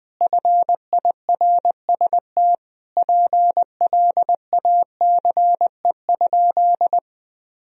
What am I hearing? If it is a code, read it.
Morse code, heard as a keyed tone, about 20 wpm, 700 Hz. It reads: FIRST PLACE?
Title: Sentences from Top 200 Words and Abbreviations 20wpm